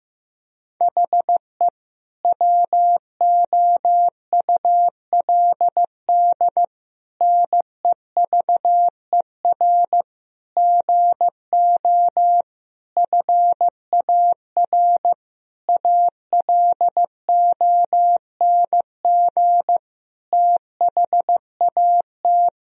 Morse code, heard as a keyed tone, about 15 wpm, 700 Hz. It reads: HE WOULD NEVER GO FAR ALONG THAT
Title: Sentences from Top 1000 Words - Review Part 2 - 15wpm